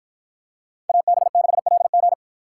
Morse code, heard as a keyed tone, about 49 wpm, 700 Hz. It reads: AB6LG